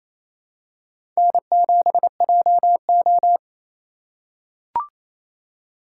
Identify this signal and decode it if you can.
Morse code, heard as a keyed tone, about 28 wpm, 700 Hz. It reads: N7JO
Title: USA Call Signs - 4 Characters - 2x Character Spacing - 28wpm